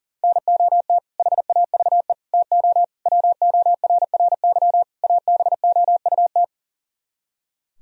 Morse code, heard as a keyed tone, about 50 wpm, 700 Hz. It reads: NOT HAVE TO WORRY ABOUT
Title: Speed-Racing - Sets of 5 Words 40wpm